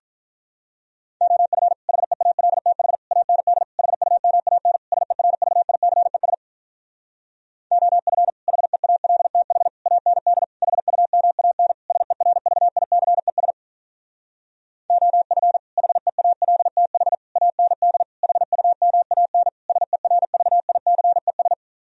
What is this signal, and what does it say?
Morse code, heard as a keyed tone, about 46 wpm, 700 Hz. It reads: OF HEALTH AND HUMAN SERVICES OF HEALTH AND HUMAN SERVICES OF HEALTH AND HUMAN SERVICES